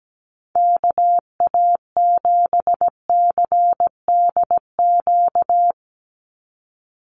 Morse code, heard as a keyed tone, about 17 wpm, 700 Hz. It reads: KA7CDQ
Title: USA Call Signs 17wpm